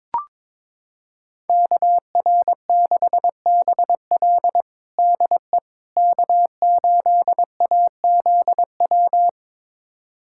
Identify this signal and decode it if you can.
Morse code, heard as a keyed tone, about 22 wpm, 700 Hz. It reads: KR6BL DE K8AZW
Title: USA Callsign DE Callsign - 5 Characters - 22wpm